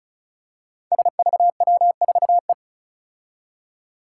Morse code, heard as a keyed tone, about 36 wpm, 700 Hz. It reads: SVW4E
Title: Five Challenging Letter-Number Combinations 35wpm